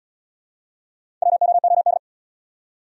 Morse code, heard as a keyed tone, about 79 wpm, 700 Hz. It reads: WOOD